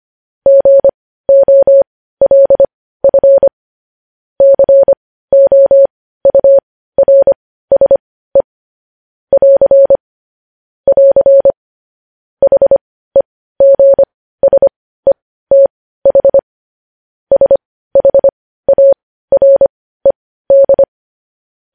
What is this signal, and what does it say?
Morse code, heard as a keyed tone, about 25 wpm, spaced as for 15 wpm, 550 Hz. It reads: GOLF COURSE <AR> <AR> HEGSETH SHARED
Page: News Headlines in Morse Code at 15 WPM